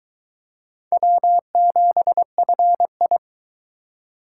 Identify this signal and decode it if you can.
Morse code, heard as a keyed tone, about 20 wpm, 700 Hz. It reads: W7FI